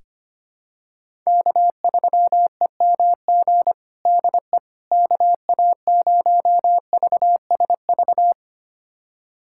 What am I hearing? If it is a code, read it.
Morse code, heard as a keyed tone, about 25 wpm, 700 Hz. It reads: K3EMG DE KA0VSV
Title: USA Callsign DE Callsign 25wpm